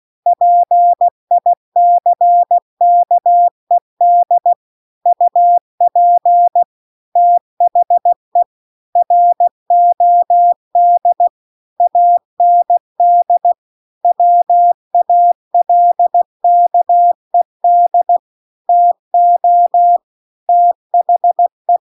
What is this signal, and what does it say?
Morse code, heard as a keyed tone, about 16 wpm, 700 Hz. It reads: PICKED UP THE ROD AND WALKED TO THE
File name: CW-The-Big-Twoheaded-River-2-16WPM